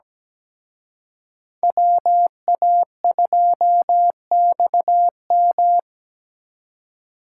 Morse code, heard as a keyed tone, about 20 wpm, 700 Hz. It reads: WA2XM
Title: Speed Racing - USA 2x2 Call Signs 17wpm